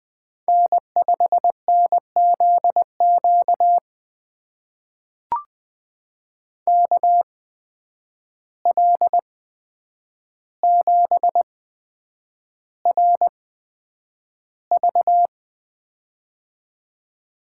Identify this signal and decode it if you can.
Morse code, heard as a keyed tone, about 20 wpm, spaced as for 9 wpm, 700 Hz. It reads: N5NZQ KL7RV
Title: USA Call Signs - 5 Characters - 8x Character Spacing - 20wpm